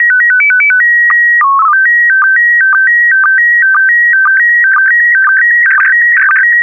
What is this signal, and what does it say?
SSTV (slow-scan television), PD 120.
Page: Vom 29. Juni bis 1. Juli 2018 waren auf 145.800 MHz (PD120) wieder SSTV-Bilder von der Internationalen Raumstation (ISS) zu empfangen.